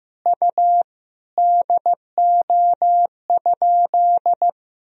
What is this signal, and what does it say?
Morse code, heard as a keyed tone, about 16 wpm, 700 Hz. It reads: U DO?
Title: Sentences from Top 100 Words and Abbreviations 15wpm